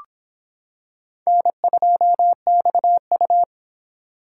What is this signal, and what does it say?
Morse code, heard as a keyed tone, about 25 wpm, 700 Hz. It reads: N2XU